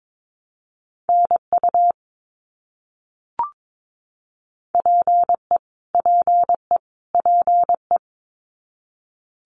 Morse code, heard as a keyed tone, about 22 wpm, 700 Hz. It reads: NU PE PE PE
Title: Canadian Provinces and Territories - Repeated 3x - 22wpm